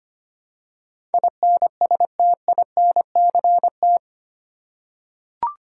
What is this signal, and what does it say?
Morse code, heard as a keyed tone, about 25 wpm, 700 Hz. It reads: INSTINCT
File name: Eight Letter Words - 25wpm